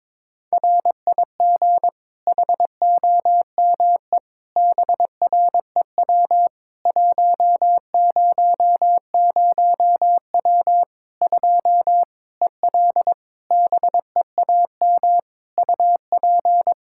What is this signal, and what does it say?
Morse code, heard as a keyed tone, about 22 wpm, 700 Hz. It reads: RIG HOME BREW 100W 2 EL BEAM UP
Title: Practice QSO - Exchange 2 - Info - 22wpm